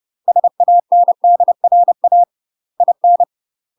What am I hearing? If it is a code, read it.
Morse code, heard as a keyed tone, about 30 wpm, 700 Hz. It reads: SANDRA IN
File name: UK MASTER QSOs TRAINING / 30WPM / UK MASTER QSOs 0004